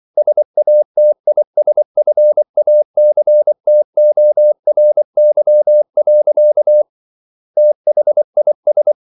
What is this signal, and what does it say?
Morse code, heard as a keyed tone, about 24 wpm, 600 Hz. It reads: SATISFACTORY. THIS